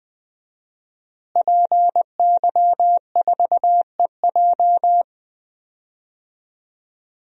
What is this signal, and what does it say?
Morse code, heard as a keyed tone, about 20 wpm, 700 Hz. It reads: PY4EJ
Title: DXCC-100 Call Signs - 5 Characters - Repeated 2x - 20wpm